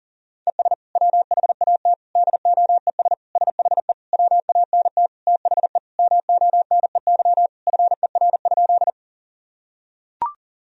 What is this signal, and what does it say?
Morse code, heard as a keyed tone, about 37 wpm, 700 Hz. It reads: ES WHAT DOES SHE WANT THE MONEY FER?
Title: Sentences from Top 400 Words and Abbreviations 40wpm